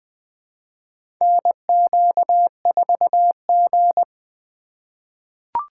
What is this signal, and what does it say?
Morse code, heard as a keyed tone, about 20 wpm, 700 Hz. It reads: NQ4G
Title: USA Call Signs - 4 Characters - 2x Character Spacing - 20wpm